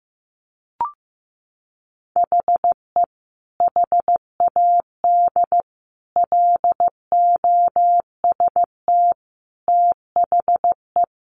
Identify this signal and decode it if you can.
Morse code, heard as a keyed tone, about 15 wpm, 700 Hz. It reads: HE HAD LOST THE
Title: Sentences from Top 1500 Words 15wpm